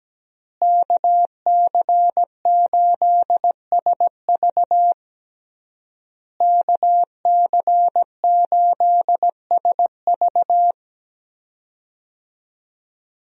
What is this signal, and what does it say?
Morse code, heard as a keyed tone, about 17 wpm, 700 Hz. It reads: KC8SV KC8SV
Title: USA Call Signs - 5 Characters - Repeated 3x - 17wpm